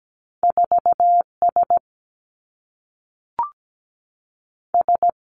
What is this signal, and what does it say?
Morse code, heard as a keyed tone, about 17 wpm, 700 Hz. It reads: 4 S S
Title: Two Challenging Letter-Number Combinations 17wpm